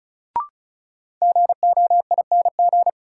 Morse code, heard as a keyed tone, about 36 wpm, 700 Hz. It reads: GOING
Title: Sets of 2 Words - New Word Drill - 35wpm